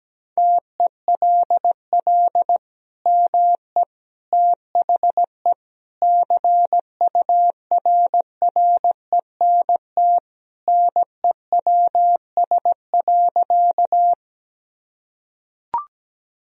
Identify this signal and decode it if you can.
Morse code, heard as a keyed tone, about 17 wpm, 700 Hz. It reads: TELL ME THE CURRENT NEWS.
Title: Sentences from Top 1000 Words - Review Part 4 - 17wpm